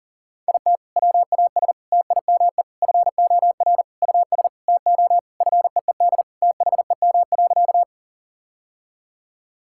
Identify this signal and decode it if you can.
Morse code, heard as a keyed tone, about 40 wpm, 700 Hz. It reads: IT WAS TIME FOR US TO FEED THEM.
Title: Sentences from Top 1600 Words 40wpm